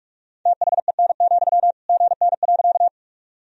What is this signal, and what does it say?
Morse code, heard as a keyed tone, about 46 wpm, 700 Hz. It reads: THEN, GN.
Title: Sentences from Top 300 Words and Abbreviations 45wpm